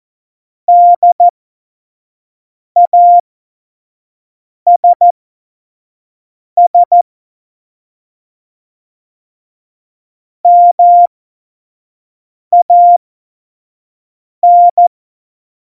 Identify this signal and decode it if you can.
Morse code, heard as a keyed tone, about 14 wpm, spaced as for 5 wpm, 700 Hz. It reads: DASS MAN
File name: Morsekurs / Witze / Witz 04. Februar 2020, 25 F70